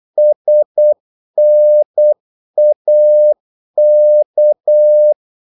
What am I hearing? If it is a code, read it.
Morse code, heard as a keyed tone, about 8 wpm, 600 Hz. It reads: SNAK
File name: Korte ord DK 8wpm